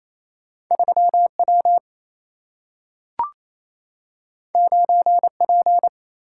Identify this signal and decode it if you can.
Morse code, heard as a keyed tone, about 28 wpm, 700 Hz. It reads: 3W 9P
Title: Single-Letter Single-Number 28wpm